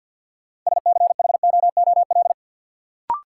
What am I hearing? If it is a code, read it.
Morse code, heard as a keyed tone, about 52 wpm, 700 Hz. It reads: SCHOOL
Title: Sets of 2 Words - New Word Drill - 50wpm